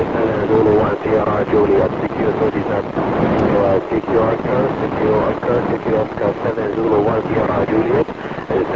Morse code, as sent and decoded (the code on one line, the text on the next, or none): none